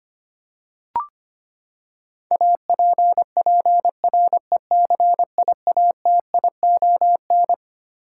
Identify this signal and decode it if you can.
Morse code, heard as a keyed tone, about 25 wpm, 700 Hz. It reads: APPRECIATION
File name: Twelve Letter Words - 25wpm